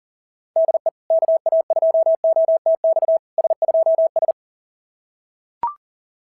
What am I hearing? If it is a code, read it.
Morse code, heard as a keyed tone, about 40 wpm, 690 Hz. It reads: DE KA2OTX S2S
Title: SOTA QSO Practice 40wpm